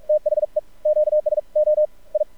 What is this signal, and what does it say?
Morse code, heard as a keyed tone, about 48 wpm, spaced as for 41 wpm, 610 Hz. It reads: THE QSO I
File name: archived / CRSnet